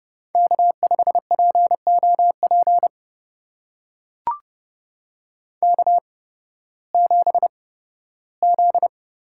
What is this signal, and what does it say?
Morse code, heard as a keyed tone, about 30 wpm, spaced as for 16 wpm, 700 Hz. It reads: K5POP K7Z
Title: USA Call Signs - 5 Characters - 8x Character Spacing - 30wpm